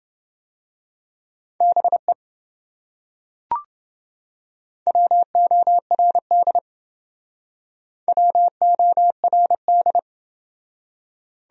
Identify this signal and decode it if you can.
Morse code, heard as a keyed tone, about 28 wpm, 700 Hz. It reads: BE WORD WORD